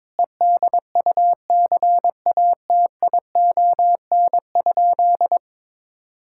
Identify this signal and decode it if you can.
Morse code, heard as a keyed tone, about 22 wpm, 700 Hz. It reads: EDUCATION?
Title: Sentences from Top 1600 Words 22wpm